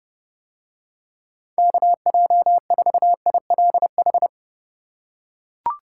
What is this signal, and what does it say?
Morse code, heard as a keyed tone, about 28 wpm, 700 Hz. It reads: KJ4ILH